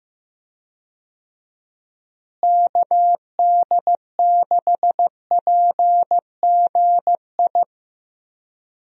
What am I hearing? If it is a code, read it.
Morse code, heard as a keyed tone, about 15 wpm, 700 Hz. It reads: KD6PGI
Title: USA Call Signs - 6 Characters - Repeated 3x - 15wpm